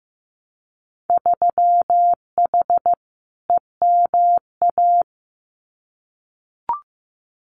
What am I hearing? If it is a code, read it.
Morse code, heard as a keyed tone, about 15 wpm, 700 Hz. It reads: 3H EMA
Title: Winter Field Day Exchanges Encore 15wpm